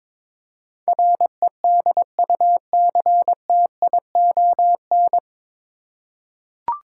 Morse code, heard as a keyed tone, about 22 wpm, 700 Hz. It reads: REDUCTION
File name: 9 Letter Words - First 3 Letters Given - 22wpm